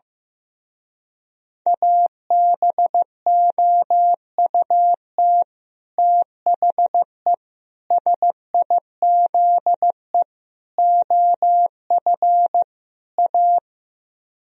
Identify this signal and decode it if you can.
Morse code, heard as a keyed tone, about 15 wpm, 700 Hz. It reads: ABOUT THE SIZE OF A
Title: Speed-Racing - Sets of 5 Words 15wpm